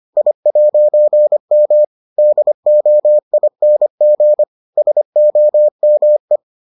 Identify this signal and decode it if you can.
Morse code, heard as a keyed tone, about 25 wpm, 600 Hz. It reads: I'M DOING SOME